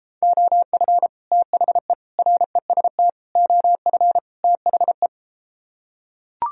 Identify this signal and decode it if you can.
Morse code, heard as a keyed tone, about 33 wpm, 700 Hz. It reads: OF THE REST OF THE